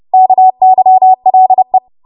Morse code, heard as a keyed tone, about 30 wpm, 750 Hz. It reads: KYLE
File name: Morse / 30WPM